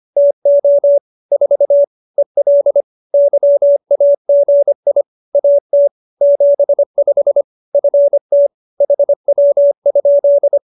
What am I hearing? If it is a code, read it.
Morse code, heard as a keyed tone, about 25 wpm, 570 Hz. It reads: TO 4 EL YAGI AT 75 FT HW?
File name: QSO302 25